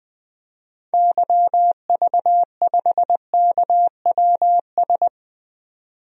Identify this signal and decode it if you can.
Morse code, heard as a keyed tone, about 20 wpm, 700 Hz. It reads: YV5KWS
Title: DXCC-100 Call Signs - 6 Characters - 20wpm